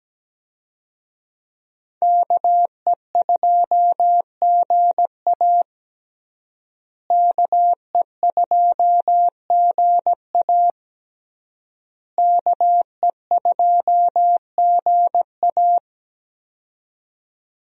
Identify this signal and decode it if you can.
Morse code, heard as a keyed tone, about 17 wpm, 700 Hz. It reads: KE2GA KE2GA KE2GA
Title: USA Call Signs - 5 Characters - Repeated 3x - 17wpm